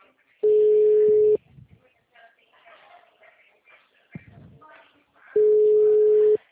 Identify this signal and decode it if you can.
Ringback tone (Europe, 425 Hz), with speech.